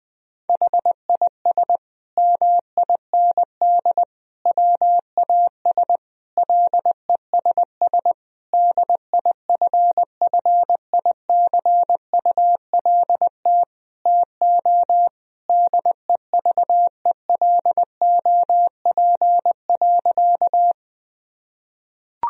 Morse code, heard as a keyed tone, about 20 wpm, 700 Hz. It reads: HIS MIND WAS LESS DIFFICULT TO DEVELOP.
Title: Sentences from Top 1000 Words - Review Part 2 - 20wpm